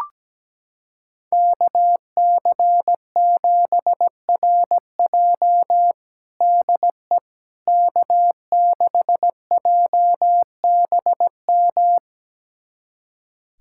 Morse code, heard as a keyed tone, about 17 wpm, 700 Hz. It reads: KC7RJ DE K6JBM
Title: USA Callsign DE Callsign - 5 Characters - 17wpm